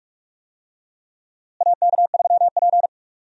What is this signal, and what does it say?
Morse code, heard as a keyed tone, about 43 wpm, 700 Hz. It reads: AK3P
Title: USA Call Signs - 4 Characters - Repeated 2x - 45wpm